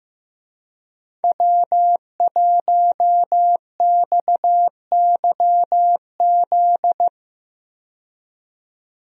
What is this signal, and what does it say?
Morse code, heard as a keyed tone, about 15 wpm, 700 Hz. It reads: W1XYZ
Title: USA Call Signs - 5 Characters - 15wpm